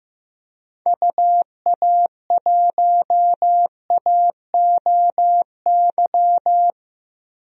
Morse code, heard as a keyed tone, about 15 wpm, 700 Hz. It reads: UA1AOY
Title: DXCC-100 Call Signs - 6 Characters - 15wpm